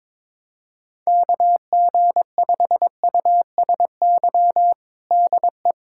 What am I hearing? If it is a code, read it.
Morse code, heard as a keyed tone, about 22 wpm, 700 Hz. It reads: KG5USY DE
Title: USA Callsign DE Callsign - 6 Characters - 22wpm